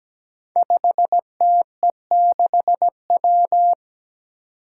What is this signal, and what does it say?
Morse code, heard as a keyed tone, about 17 wpm, 700 Hz. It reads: 5TE6W
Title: Five Challenging Letter-Number Combinations 17wpm